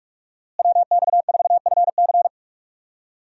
Morse code, heard as a keyed tone, about 42 wpm, 700 Hz. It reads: WX4FC
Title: USA Call Signs - 5 Characters - Repeated 3x - 45wpm